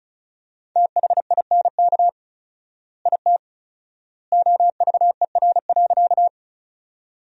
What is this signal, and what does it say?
Morse code, heard as a keyed tone, about 34 wpm, 700 Hz. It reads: THINK IT OVER.
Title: 4x Word-Spacing - Sentences from Top 200 Words - 35wpm